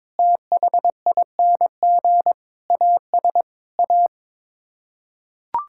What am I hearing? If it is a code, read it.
Morse code, heard as a keyed tone, about 22 wpm, 700 Hz. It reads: THING AS A
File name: practice / 22wpm / Sets of 8 Words - 22wpm